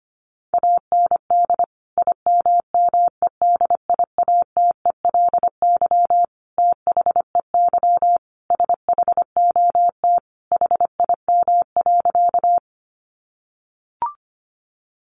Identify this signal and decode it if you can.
Morse code, heard as a keyed tone, about 25 wpm, 700 Hz. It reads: AND IMMEDIATELY THEY SHOT HIM.
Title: Sentences from Top 1100 Words 25wpm